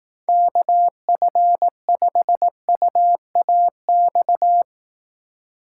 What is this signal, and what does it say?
Morse code, heard as a keyed tone, about 18 wpm, 700 Hz. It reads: KF5UAX
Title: Speed Racing - USA 2x3 Call Signs 15wpm